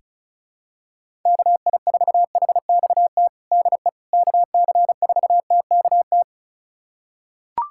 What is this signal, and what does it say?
Morse code, heard as a keyed tone, about 36 wpm, 700 Hz. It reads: KI4HXT DE KC4TKT
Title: USA Callsign DE Callsign - 6 Characters - 35wpm